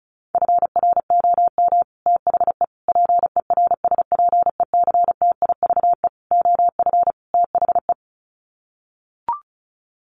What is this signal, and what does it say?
Morse code, heard as a keyed tone, about 34 wpm, 700 Hz. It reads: FROM THE PERSPECTIVE OF THE
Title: Sets of 5 Words Encore2 35wpm